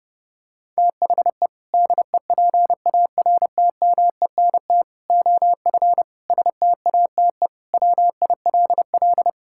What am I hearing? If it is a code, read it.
Morse code, heard as a keyed tone, about 30 wpm, 700 Hz. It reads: THE DEPARTMENT OF STATE WILL
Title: Sentences from Top 1000 Words - Review Part 5 - 30wpm